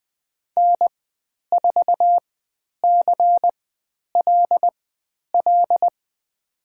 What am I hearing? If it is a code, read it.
Morse code, heard as a keyed tone, about 20 wpm, spaced as for 10 wpm, 700 Hz. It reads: N4CLL
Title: 109 Lesson 18g - Letter m - Call Signs 20wpm